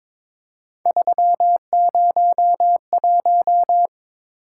Morse code, heard as a keyed tone, about 23 wpm, 700 Hz. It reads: 301
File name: practice / 22wpm / Three Letter-Number Combinations - 22wpm